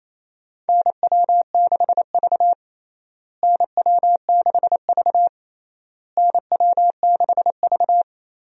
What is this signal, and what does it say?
Morse code, heard as a keyed tone, about 28 wpm, 700 Hz. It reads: NW6V NW6V NW6V
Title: USA Call Signs - 4 Characters - Repeated 3x - 28wpm